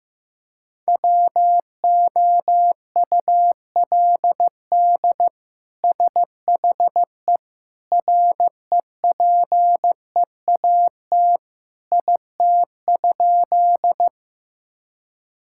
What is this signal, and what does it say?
Morse code, heard as a keyed tone, about 16 wpm, 700 Hz. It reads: WOULD SHE REPEAT IT?
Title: Sentences from Top 1600 Words 15wpm